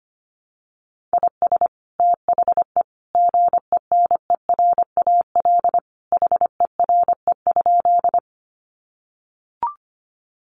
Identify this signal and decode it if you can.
Morse code, heard as a keyed tone, about 25 wpm, 700 Hz. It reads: IS THE GENERAL HERE?
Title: Sentences from Top 600 Words 25wpm